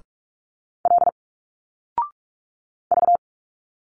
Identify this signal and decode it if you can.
Morse code, heard as a keyed tone, about 46 wpm, 700 Hz. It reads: L V
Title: Single Letters 45wpm